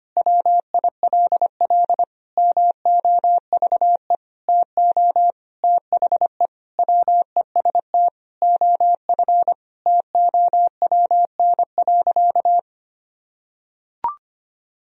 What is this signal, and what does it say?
Morse code, heard as a keyed tone, about 25 wpm, 700 Hz. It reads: WILL MOVE TO THE WEST OF TOWN.
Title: Sentences from Top 1000 Words - Review Part 2 - 25wpm